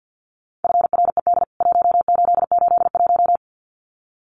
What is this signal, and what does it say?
Morse code, heard as a keyed tone, about 48 wpm, 700 Hz. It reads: FRED 1881
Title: CWT Exchange 50wpm